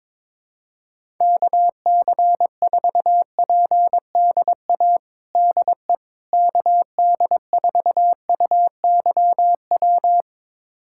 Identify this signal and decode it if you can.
Morse code, heard as a keyed tone, about 22 wpm, 700 Hz. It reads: KC4PDA DE KD4UYW
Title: USA Callsign DE Callsign - 6 Characters - 22wpm